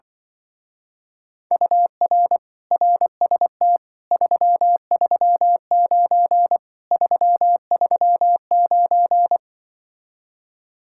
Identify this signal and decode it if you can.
Morse code, heard as a keyed tone, about 24 wpm, 700 Hz. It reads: UR RST 339 339
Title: Speed-Racing Common QSO Phrases - 20wpm